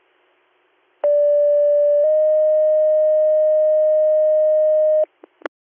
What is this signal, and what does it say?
Two-tone page, heard as a one-second tone, then a three-second tone, about 580 then 620 Hz. It is heard from 1.0 to 5.0 s.